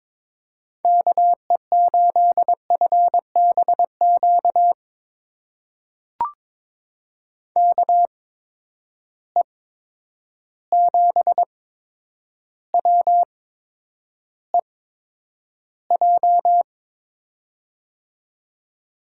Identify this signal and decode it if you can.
Morse code, heard as a keyed tone, about 22 wpm, spaced as for 10 wpm, 700 Hz. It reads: KE8FBQ KE7WEJ
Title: USA Call Signs - 6 Characters - 8x Character Spacing - 22wpm